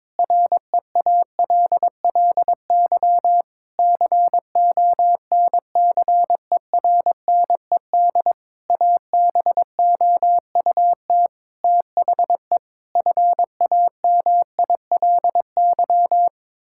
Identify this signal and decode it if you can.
Morse code, heard as a keyed tone, about 22 wpm, 700 Hz. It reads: REALLY CONCERNED ABOUT THE FAMILY
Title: Sentences from Top 1000 Words - Review Part 6 - 22wpm